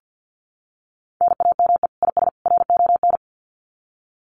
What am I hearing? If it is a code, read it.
Morse code, heard as a keyed tone, about 50 wpm, 700 Hz. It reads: NAME IS RON
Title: Common QSO Phrases 50wpm